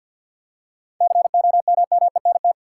Morse code, heard as a keyed tone, about 52 wpm, 700 Hz. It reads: COMMENT
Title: Seven Letter Words - 50wpm